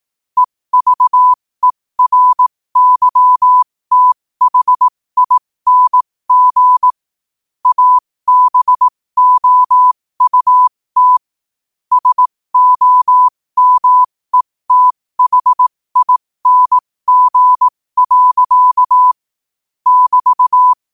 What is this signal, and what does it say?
Morse code, heard as a keyed tone, about 18 wpm, spaced as for 15 wpm, 1000 Hz. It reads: EVERYTHING ABOUT SOMETHING. <BT>
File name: qotdfiles / 15wpm